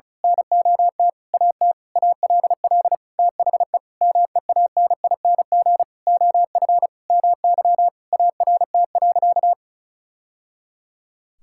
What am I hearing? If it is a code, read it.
Morse code, heard as a keyed tone, about 36 wpm, 700 Hz. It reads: NOT AT ALL THE MEANING OF MY ART.
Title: Sentences from Top 1000 Words - Review Part 6 - 35wpm